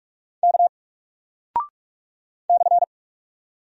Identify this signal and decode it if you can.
Morse code, heard as a keyed tone, about 46 wpm, 700 Hz. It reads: K /
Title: Single Character (Letter:Number:Punctuation:Prosign) 45wpm